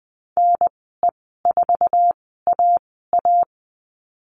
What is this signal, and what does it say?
Morse code, heard as a keyed tone, about 20 wpm, spaced as for 14 wpm, 700 Hz. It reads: NE4AA
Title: 069 Lesson 12g - Letter d - Call Signs 20wpm